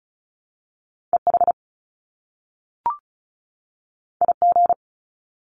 Morse code, heard as a keyed tone, about 36 wpm, 700 Hz. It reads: EH IG
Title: Top Two Letter Combinations in Words 35wpm